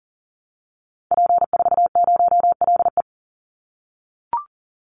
Morse code, heard as a keyed tone, about 40 wpm, 700 Hz. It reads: P40LE
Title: Call Signs - 40wpm